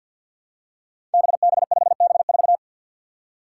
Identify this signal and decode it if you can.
Morse code, heard as a keyed tone, about 53 wpm, 700 Hz. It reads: BB5B4
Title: Five Challenging Letter-Number Combinations 50wpm